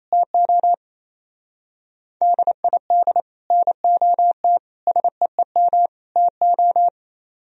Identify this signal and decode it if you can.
Morse code, heard as a keyed tone, about 33 wpm, 700 Hz. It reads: TO DID NOT SEEM TO
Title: Speed-Racing - Sets of 4 Words 28wpm